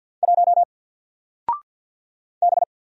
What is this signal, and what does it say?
Morse code, heard as a keyed tone, about 50 wpm, 700 Hz. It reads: 1 B
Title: Single Letter-Number Alt 50wpm